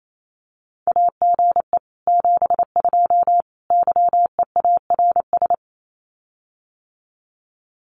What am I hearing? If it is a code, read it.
Morse code, heard as a keyed tone, about 28 wpm, 700 Hz. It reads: AGE 72 YEARS
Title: Common QSO Phrases Encore 28wpm